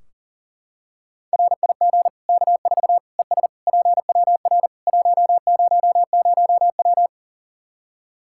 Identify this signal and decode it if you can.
Morse code, heard as a keyed tone, about 40 wpm, 700 Hz. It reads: RIG K4 ES PWR 100W
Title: Practice QSO - Exchange 2 - Info - 40wpm